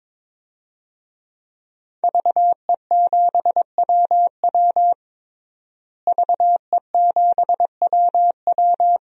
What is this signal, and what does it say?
Morse code, heard as a keyed tone, about 22 wpm, 700 Hz. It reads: VE7WW VE7WW
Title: DXCC-100 Call Signs - 5 Characters - Repeated 3x - 22wpm